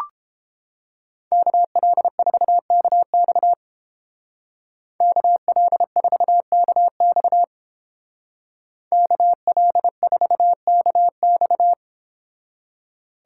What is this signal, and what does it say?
Morse code, heard as a keyed tone, about 28 wpm, 700 Hz. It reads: KL4KX KL4KX KL4KX